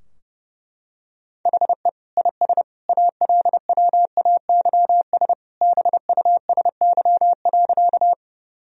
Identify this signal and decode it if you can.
Morse code, heard as a keyed tone, about 32 wpm, 700 Hz. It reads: HE IS ALWAYS BUSY.
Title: Sentences from Top 1000 Words - Review Part 3 - 30wpm